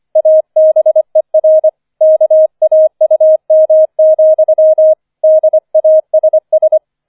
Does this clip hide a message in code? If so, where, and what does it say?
Yes: Morse code from 0.2 to 6.8 s, heard as a keyed tone, about 25 wpm, 630 Hz. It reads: ABER KAUM, DASS